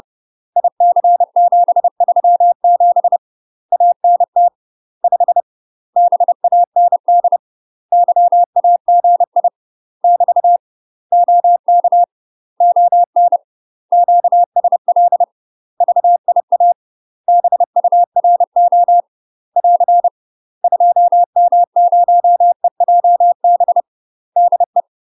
Morse code, heard as a keyed tone, about 30 wpm, 700 Hz. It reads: IC737 ANT 5 BAND YAGI <BT> OK ON QSL VIA BURO <AR> 2M0EJB DE
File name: UK MASTER QSOs TRAINING/30WPM